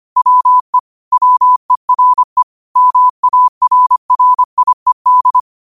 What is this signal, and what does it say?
Morse code, heard as a keyed tone, about 25 wpm, 1000 Hz. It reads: WE WERE MARRIED
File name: qotdfiles / 25wpm